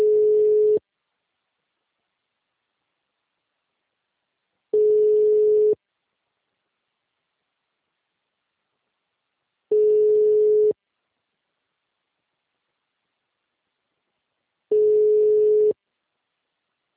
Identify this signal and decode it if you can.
Ringback tone (Europe, 425 Hz).
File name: out- -2013- - - .1913